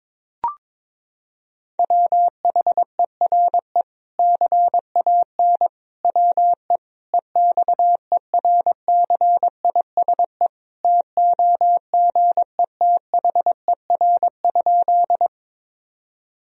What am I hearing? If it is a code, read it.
Morse code, heard as a keyed tone, about 22 wpm, 700 Hz. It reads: WHERE CAN WE EXERCISE TOGETHER?
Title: Sentences from Top 1000 Words - Review Part 4 - 22wpm